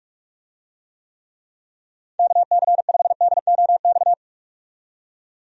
Morse code, heard as a keyed tone, about 46 wpm, 700 Hz. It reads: KC5DOX
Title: USA Call Signs - 6 Characters - 45wpm